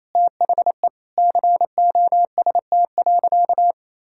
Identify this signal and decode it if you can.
Morse code, heard as a keyed tone, about 28 wpm, 700 Hz. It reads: THE COST.
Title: Sentences from Top 1000 Words - Review Part 5 - 28wpm